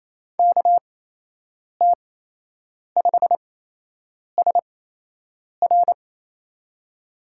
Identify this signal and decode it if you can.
Morse code, heard as a keyed tone, about 28 wpm, spaced as for 8 wpm, 700 Hz. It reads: KT5SR
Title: USA Call Signs - 5 Characters - 8x Character Spacing - 28wpm